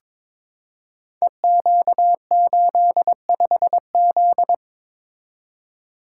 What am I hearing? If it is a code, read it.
Morse code, heard as a keyed tone, about 22 wpm, 700 Hz. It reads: EQ85Z
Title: Code Groups - 5 Letter:Numbers - 22wpm